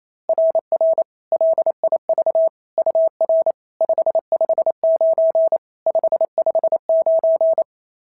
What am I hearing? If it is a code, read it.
Morse code, heard as a keyed tone, about 25 wpm, 650 Hz. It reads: RR LIV UR 559 559
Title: SOTA QSO Practice 28wpm